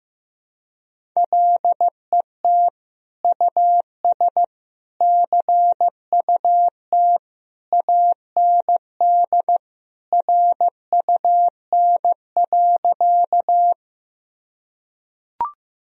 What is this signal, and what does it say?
Morse code, heard as a keyed tone, about 15 wpm, 700 Hz. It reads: LET US CUT AND RUN.
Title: Sentences from Top 1000 Words - Review Part 1 - 15wpm